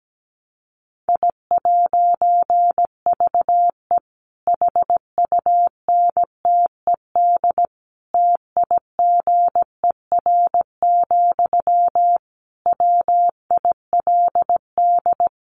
Morse code, heard as a keyed tone, about 17 wpm, 700 Hz. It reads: I'VE HUNTED TIGER, WILD
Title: A Sound Of Thunder 17wpm